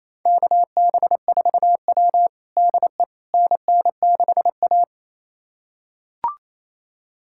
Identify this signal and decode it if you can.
Morse code, heard as a keyed tone, about 28 wpm, 700 Hz. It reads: KB4W DE NN6A
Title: USA Callsign DE Callsign - 4 Characters - 28wpm